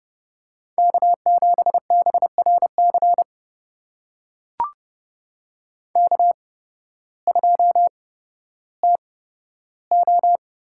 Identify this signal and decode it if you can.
Morse code, heard as a keyed tone, about 30 wpm, spaced as for 14 wpm, 700 Hz. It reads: K7BRC K2TO
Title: USA Call Signs - 5 Characters - 8x Character Spacing - 30wpm